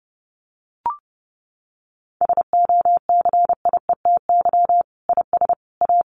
Morse code, heard as a keyed tone, about 30 wpm, 700 Hz. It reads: SOCIETY IS A
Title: Sentences from Top 1500 Words 30wpm